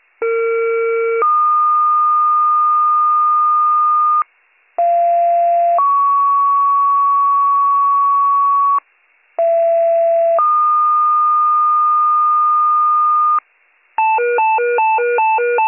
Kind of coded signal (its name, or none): two-tone page